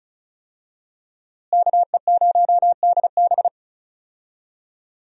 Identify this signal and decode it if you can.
Morse code, heard as a keyed tone, about 36 wpm, 700 Hz. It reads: KE0DB
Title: USA Call Signs - 5 Characters - Repeated 3x - 35wpm